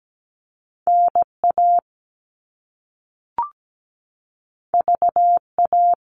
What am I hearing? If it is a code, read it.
Morse code, heard as a keyed tone, about 17 wpm, 700 Hz. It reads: NA VA
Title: All Cut Numbers - 2-Digit 17wpm